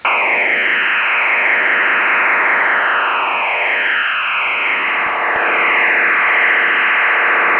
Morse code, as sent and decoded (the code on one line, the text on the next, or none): none